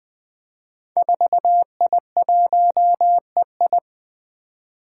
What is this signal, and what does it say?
Morse code, heard as a keyed tone, about 20 wpm, 700 Hz. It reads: 4I1EI
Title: Five Challenging Letter-Number Combinations 20wpm